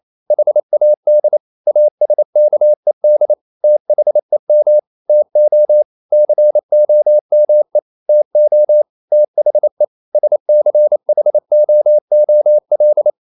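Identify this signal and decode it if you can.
Morse code, heard as a keyed tone, about 28 wpm, 600 Hz. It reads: HAD ASKED THEM TO COME TO THE SCHOOL
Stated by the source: Morse code practice at 28 wpm, text of Ken Follett's novel Never in English, from CW med Gnister.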